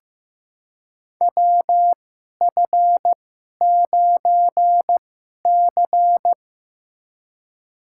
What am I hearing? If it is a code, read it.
Morse code, heard as a keyed tone, about 15 wpm, spaced as for 11 wpm, 700 Hz. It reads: WF9C
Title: USA Call Signs - 4 Characters - 2x Character Spacing - 15wpm